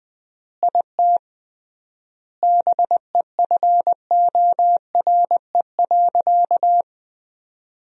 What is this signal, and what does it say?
Morse code, heard as a keyed tone, about 19 wpm, 700 Hz. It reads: IT BEFORE.
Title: 3x Word-Spacing - Sentences from Top 400 Words - 20wpm